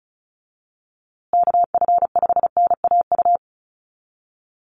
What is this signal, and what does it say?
Morse code, heard as a keyed tone, about 34 wpm, 700 Hz. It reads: KF5NAU
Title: USA Call Signs Repeated 2x 35wpm